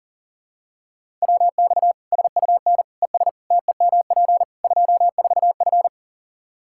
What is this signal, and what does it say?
Morse code, heard as a keyed tone, about 40 wpm, 700 Hz. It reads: WX SUN ES TEMP 24F
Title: Practice QSO - Exchange 2 - Info - 40wpm